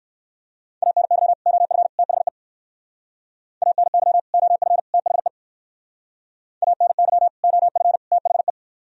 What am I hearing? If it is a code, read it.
Morse code, heard as a keyed tone, about 58 wpm, 700 Hz. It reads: ANY OF THE ANY OF THE ANY OF THE